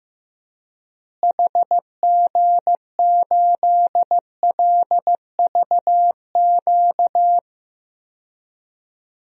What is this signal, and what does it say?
Morse code, heard as a keyed tone, about 15 wpm, 700 Hz. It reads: HG8LVQ
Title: DXCC-100 Call Signs - 6 Characters - Repeated 3x - 15wpm